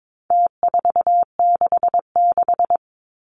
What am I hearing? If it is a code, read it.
Morse code, heard as a keyed tone, about 22 wpm, 700 Hz. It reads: T466
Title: Numbers - Contest Style - 4-Digit 22wpm